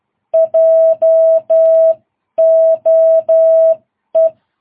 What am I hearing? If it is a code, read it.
Morse code, heard as a keyed tone, about 12 wpm, spaced as for 10 wpm, 660 Hz. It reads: JOE
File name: jh8cbh / 2022cw